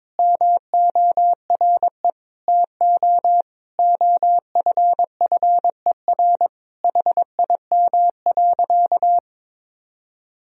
Morse code, heard as a keyed tone, about 22 wpm, 700 Hz. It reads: MORE TO OFFER HIM.
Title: Sentences from Top 1000 Words - Review Part 4 - 22wpm